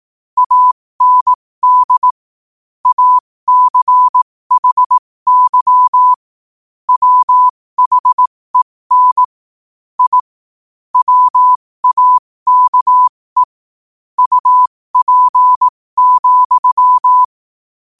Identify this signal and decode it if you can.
Morse code, heard as a keyed tone, about 18 wpm, spaced as for 15 wpm, 1000 Hz. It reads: AND ACHY WHEN I WAKE UP,
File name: qotdfiles / 15wpm